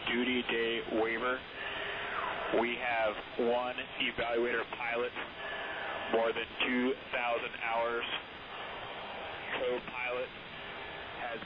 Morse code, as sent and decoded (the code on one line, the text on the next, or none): none